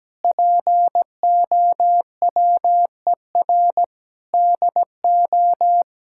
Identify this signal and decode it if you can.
Morse code, heard as a keyed tone, about 17 wpm, 700 Hz. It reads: POWER DO
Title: Sentences from Top 1000 Words - Review Part 6 - 17wpm